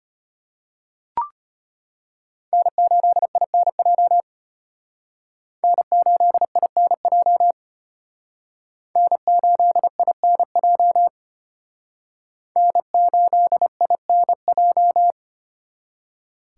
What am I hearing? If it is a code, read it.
Morse code, heard as a keyed tone, about 34 wpm, 700 Hz. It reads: N8INJ N8INJ N8INJ N8INJ